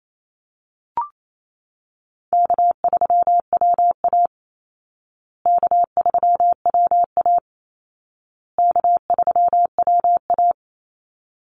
Morse code, heard as a keyed tone, about 28 wpm, 700 Hz. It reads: K3WA K3WA K3WA
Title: Easy Call Signs Repeated 3x - 28wpm